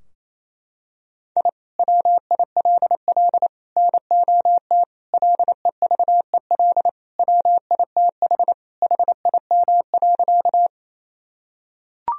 Morse code, heard as a keyed tone, about 28 wpm, 700 Hz. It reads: I WILL NOT LEVEL WITH HIM.
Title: Sentences from Top 1000 Words - Review Part 4 - 28wpm